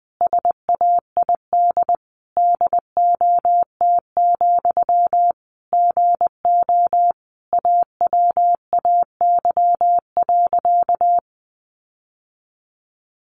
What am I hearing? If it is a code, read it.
Morse code, heard as a keyed tone, about 19 wpm, 700 Hz. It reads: SAID DOT, GO AWAY.
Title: Sentences from Top 1500 Words 20wpm